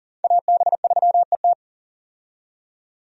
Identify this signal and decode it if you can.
Morse code, heard as a keyed tone, about 40 wpm, 700 Hz. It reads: AB3ET
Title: USA Call Signs - 5 Characters - 40wpm